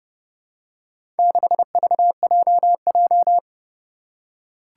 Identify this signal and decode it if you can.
Morse code, heard as a keyed tone, about 30 wpm, 690 Hz. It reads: 6VJJ
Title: Four Challenging Letter-Number Combinations 30wpm